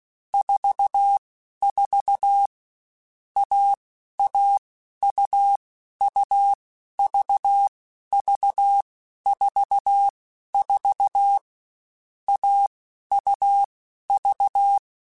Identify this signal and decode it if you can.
Morse code, heard as a keyed tone, about 16 wpm, 800 Hz. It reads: 4 4 A A U U V V 4 4 A U V